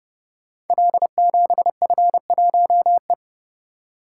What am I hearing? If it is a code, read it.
Morse code, heard as a keyed tone, about 30 wpm, 700 Hz. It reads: L7F1E
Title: Code Groups - 5 Letter:Numbers - 30wpm